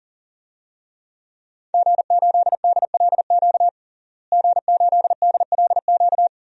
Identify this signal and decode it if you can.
Morse code, heard as a keyed tone, about 40 wpm, 700 Hz. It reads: G8DLQ G8DLQ
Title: DXCC-100 Call Signs - 5 Characters - Repeated 2x - 40wpm